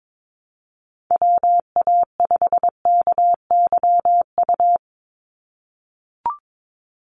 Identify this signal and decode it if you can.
Morse code, heard as a keyed tone, about 22 wpm, 700 Hz. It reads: WA5KYU
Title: USA Call Signs 22wpm